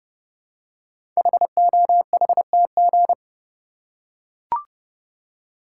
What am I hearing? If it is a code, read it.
Morse code, heard as a keyed tone, about 28 wpm, 700 Hz. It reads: HOHTG
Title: Code Groups - 5 Letters - 30wpm